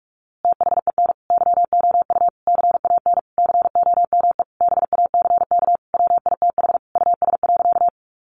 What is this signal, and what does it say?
Morse code, heard as a keyed tone, about 46 wpm, 700 Hz. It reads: THEN YOU CAN COME BACK WITH US.
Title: A Sound Of Thunder 45wpm